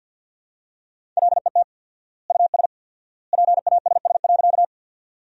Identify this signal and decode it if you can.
Morse code, heard as a keyed tone, about 50 wpm, 700 Hz. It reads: LET US PASS.
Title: 4x Word-Spacing - Sentences from Top 400 Words - 50wpm